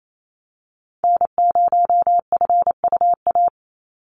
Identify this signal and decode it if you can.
Morse code, heard as a keyed tone, about 28 wpm, 700 Hz. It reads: N0FUA
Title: USA Call Signs 28wpm